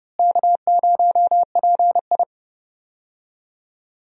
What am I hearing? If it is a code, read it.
Morse code, heard as a keyed tone, about 30 wpm, 700 Hz. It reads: K0PI
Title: USA Call Signs - 4 Characters - Repeated 3x - 30wpm